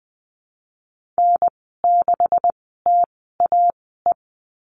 Morse code, heard as a keyed tone, about 20 wpm, spaced as for 14 wpm, 700 Hz. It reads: N6TAE
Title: 154 Lesson 25g - Letter p - Call Signs 20wpm